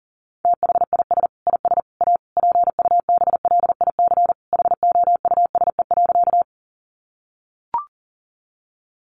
Morse code, heard as a keyed tone, about 40 wpm, 700 Hz. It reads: THIS IS A PUBLIC HOUSE.
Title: Sentences from Top 600 Words 40wpm